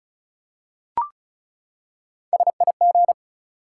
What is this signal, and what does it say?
Morse code, heard as a keyed tone, about 36 wpm, 700 Hz. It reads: SIG
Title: Top 180 QSO Elements 35wpm